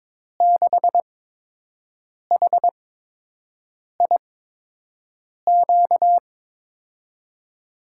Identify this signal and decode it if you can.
Morse code, heard as a keyed tone, about 22 wpm, spaced as for 6 wpm, 700 Hz. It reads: 6HIQ
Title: USA Call Signs - 6 Characters - 8x Character Spacing - 22wpm